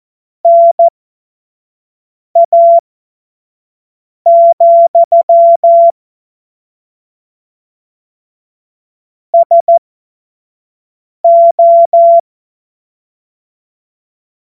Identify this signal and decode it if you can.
Morse code, heard as a keyed tone, about 14 wpm, spaced as for 5 wpm, 680 Hz. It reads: NA, SO
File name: Morsekurs / Witze / Witz 13 Februar 2024 25 F70